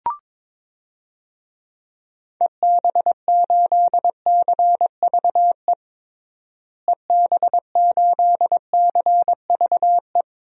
Morse code, heard as a keyed tone, about 22 wpm, 700 Hz. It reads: EB8CVE EB8CVE
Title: DXCC-100 Call Signs - 6 Characters - Repeated 3x - 22wpm